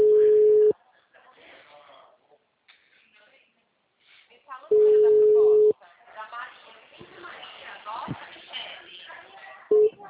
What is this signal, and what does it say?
Ringback tone (Europe, 425 Hz), with speech.
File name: out- -2013- - - .2100